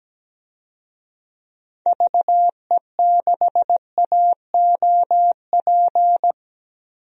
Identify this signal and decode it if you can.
Morse code, heard as a keyed tone, about 17 wpm, 700 Hz. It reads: VE6AOP
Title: DXCC-100 Call Signs - 6 Characters - Repeated 2x - 17wpm